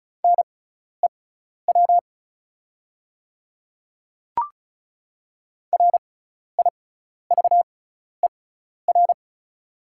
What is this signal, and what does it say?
Morse code, heard as a keyed tone, about 36 wpm, spaced as for 12 wpm, 690 Hz. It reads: NEW RIVER
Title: 6x Character Spacing - Top 300 Words - 35wpm